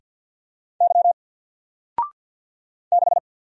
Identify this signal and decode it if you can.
Morse code, heard as a keyed tone, about 48 wpm, 700 Hz. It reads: Y 6
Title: Single Character (Letter:Number:Punctuation:Prosign) Alt 50wpm